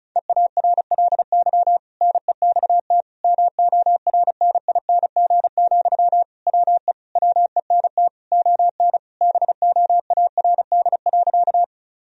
Morse code, heard as a keyed tone, about 36 wpm, 700 Hz. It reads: EARLY NEXT MORNING, WE WENT ON BOARD.
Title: Sentences from Top 1000 Words - Review Part 4 - 35wpm